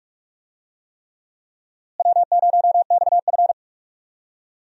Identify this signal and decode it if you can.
Morse code, heard as a keyed tone, about 46 wpm, 700 Hz. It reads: W0XF
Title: USA Call Signs - 4 Characters - Repeated 3x - 45wpm